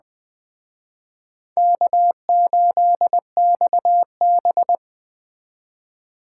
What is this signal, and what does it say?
Morse code, heard as a keyed tone, about 19 wpm, 700 Hz. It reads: K8XB